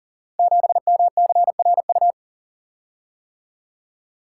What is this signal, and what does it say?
Morse code, heard as a keyed tone, about 38 wpm, 690 Hz. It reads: 7MCRU
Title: Code Groups - 5 Letter:Numbers - 40wpm